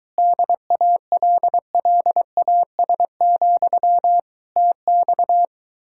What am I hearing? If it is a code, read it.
Morse code, heard as a keyed tone, about 22 wpm, 700 Hz. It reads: DALLAS, TX